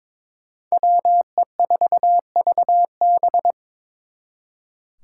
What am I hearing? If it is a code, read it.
Morse code, heard as a keyed tone, about 26 wpm, 700 Hz. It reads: WE4VB
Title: Speed Racing - USA 2x2 Call Signs 22wpm